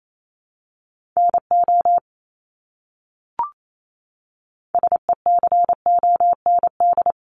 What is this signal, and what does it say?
Morse code, heard as a keyed tone, about 28 wpm, 700 Hz. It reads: NO SECOND
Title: Top 500 Words Repeated 28wpm